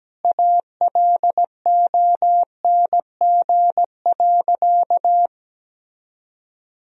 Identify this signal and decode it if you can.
Morse code, heard as a keyed tone, about 17 wpm, 700 Hz. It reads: ALONG.
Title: Sentences from Top 1000 Words - Review Part 3 - 17wpm